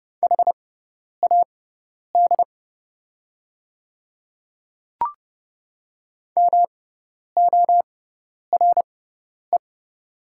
Morse code, heard as a keyed tone, about 30 wpm, spaced as for 10 wpm, 700 Hz. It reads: HAD MORE
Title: 6x Character Spacing - Top 100 Words - 30wpm